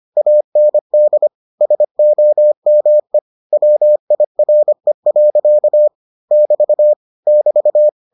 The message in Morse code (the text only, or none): AND SOME WIRE. <BT> <BT>